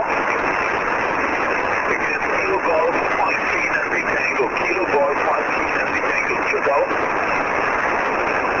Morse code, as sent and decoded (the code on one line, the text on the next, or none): none